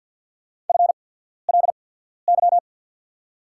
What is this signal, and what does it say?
Morse code, heard as a keyed tone, about 54 wpm, spaced as for 14 wpm, 690 Hz. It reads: FLY
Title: 8x Character Spacing - Top 500 Words - 50wpm